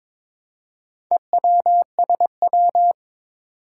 Morse code, heard as a keyed tone, about 22 wpm, 700 Hz. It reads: EWSW
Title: Four Challenging Letter-Number Combinations 22wpm